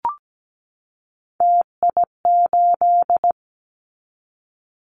Morse code, heard as a keyed tone, about 17 wpm, 700 Hz. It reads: TI8
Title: Callsign Prefixes for Top 100 DXCC - 17wpm